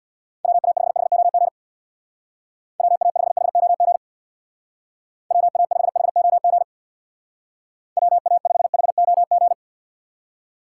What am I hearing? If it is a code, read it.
Morse code, heard as a keyed tone, about 62 wpm, 700 Hz. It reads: WA5HOG WA5HOG WA5HOG WA5HOG
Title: Speed Racing - USA 2x3 Call Signs 50wpm